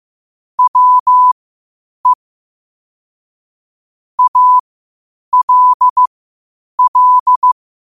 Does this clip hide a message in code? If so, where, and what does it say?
Yes: Morse code from 0.6 to 7.5 s, heard as a keyed tone, about 15 wpm, spaced as for 8 wpm, 1000 Hz. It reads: WE ALL